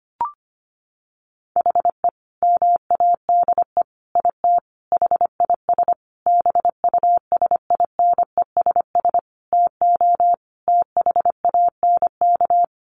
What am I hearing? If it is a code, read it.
Morse code, heard as a keyed tone, about 25 wpm, 700 Hz. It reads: HE MADE IT HIS BUSINESS TO THANK
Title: Sentences from Top 1400 Words 25wpm